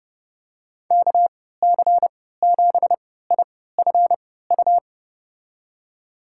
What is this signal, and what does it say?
Morse code, heard as a keyed tone, about 30 wpm, spaced as for 17 wpm, 700 Hz. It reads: KC7IFU
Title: USA Call Signs - 6 Characters - 3x Character Spacing - 30wpm